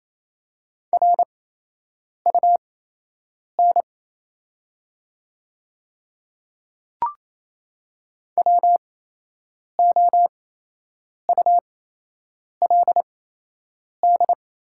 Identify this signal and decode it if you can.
Morse code, heard as a keyed tone, about 28 wpm, spaced as for 8 wpm, 700 Hz. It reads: RUN WOULD
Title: 8x Character Spacing - Top 300 Words - 28wpm